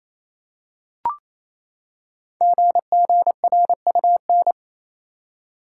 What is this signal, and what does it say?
Morse code, heard as a keyed tone, about 27 wpm, 700 Hz. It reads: GGRUN
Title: Code Groups - 5 Letters - 28wpm